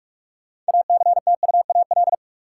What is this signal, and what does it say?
Morse code, heard as a keyed tone, about 46 wpm, 700 Hz. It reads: ACTUAL
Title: Sentences from Top 1000 Words - Review Part 5 - New Word Drill - 45wpm